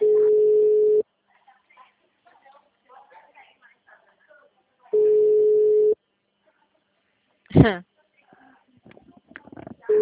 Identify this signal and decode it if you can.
Ringback tone (Europe, 425 Hz), with speech.
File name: out- -2018- - - .943